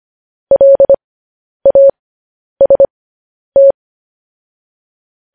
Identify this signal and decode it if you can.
Morse code, heard as a keyed tone, about 25 wpm, spaced as for 10 wpm, 550 Hz. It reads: LAST
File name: podcast / News headlines-1x-25x10